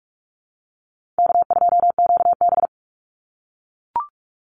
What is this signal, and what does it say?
Morse code, heard as a keyed tone, about 44 wpm, 700 Hz. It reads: K2QB
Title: Easy Call Signs - 45wpm